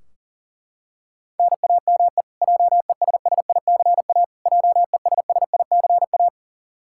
Morse code, heard as a keyed tone, about 40 wpm, 700 Hz. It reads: NAME JESSICA JESSICA
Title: Practice QSO - Exchange 1 - Beginning - 40wpm